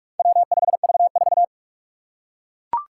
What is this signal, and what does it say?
Morse code, heard as a keyed tone, about 44 wpm, 700 Hz. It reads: W5V4
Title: Four Challenging Letter-Number Combinations 45wpm